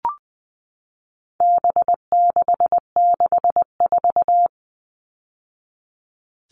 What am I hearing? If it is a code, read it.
Morse code, heard as a keyed tone, about 20 wpm, 700 Hz. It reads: B664
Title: All Cut Numbers - 4-Digit 20wpm